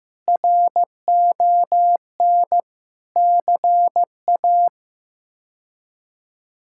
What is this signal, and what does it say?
Morse code, heard as a keyed tone, about 15 wpm, 700 Hz. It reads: RON CA
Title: K1USN SST QSO Practice - 15wpm